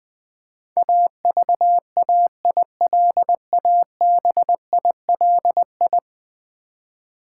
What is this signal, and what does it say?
Morse code, heard as a keyed tone, about 20 wpm, 700 Hz. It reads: AVAILABILI
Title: Build A Word - 5 Letters At A Time - 12-Letter Words - 20wpm